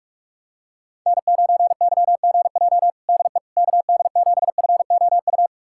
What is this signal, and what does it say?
Morse code, heard as a keyed tone, about 46 wpm, 700 Hz. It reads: N9YGJ DE KD7FOU
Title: USA Callsign DE Callsign 45wpm